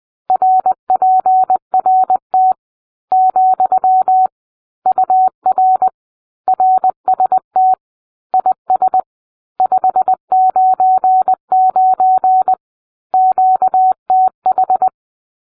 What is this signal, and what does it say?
Morse code, heard as a keyed tone, about 20 wpm, 750 Hz. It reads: RPRT , UR RST IS 599 QTH